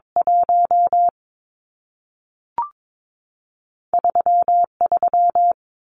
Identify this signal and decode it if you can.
Morse code, heard as a keyed tone, about 22 wpm, 700 Hz. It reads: 1 3 3
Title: Numbers - 2-Digit 22wpm